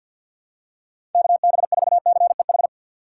Morse code, heard as a keyed tone, about 46 wpm, 700 Hz. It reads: KB4CEH
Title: USA Call Signs - 6 Characters - 6x Character Spacing - 50wpm